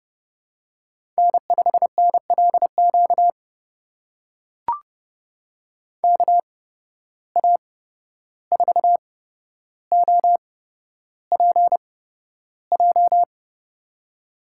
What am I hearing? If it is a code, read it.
Morse code, heard as a keyed tone, about 30 wpm, spaced as for 12 wpm, 700 Hz. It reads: N5NLQ KA4OPJ
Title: USA Call Signs - 8x Character Spacing - 30wpm